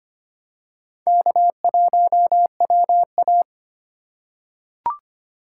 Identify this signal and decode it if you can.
Morse code, heard as a keyed tone, about 25 wpm, 700 Hz. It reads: K1WA
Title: USA Call Signs - 2x Character Spacing - 25wpm